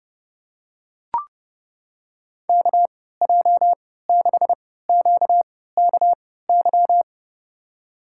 Morse code, heard as a keyed tone, about 30 wpm, spaced as for 17 wpm, 700 Hz. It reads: KJ6QKY
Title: USA Call Signs - 6 Characters - 3x Character Spacing - 30wpm